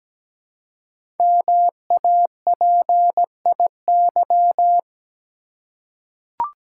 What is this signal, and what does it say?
Morse code, heard as a keyed tone, about 17 wpm, 700 Hz. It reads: MAPIY
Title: Code Groups - 5 Letters - 17wpm